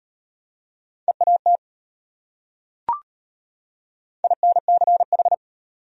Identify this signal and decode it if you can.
Morse code, heard as a keyed tone, about 31 wpm, 700 Hz. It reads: EAT INCH
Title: Speed-Racing Top 500 Words 25wpm